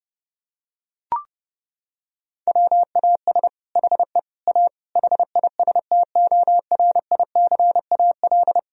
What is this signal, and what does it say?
Morse code, heard as a keyed tone, about 29 wpm, 700 Hz. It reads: WAS HE A HISTORICAL
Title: Sentences from Top 1000 Words - Review Part 4 - 30wpm